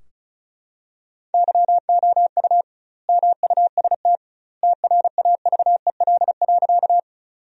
Morse code, heard as a keyed tone, about 36 wpm, 700 Hz. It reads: YOU MUST TRAVEL.
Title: 2x Word-Spacing - Sentences from Top 400 Words - 35wpm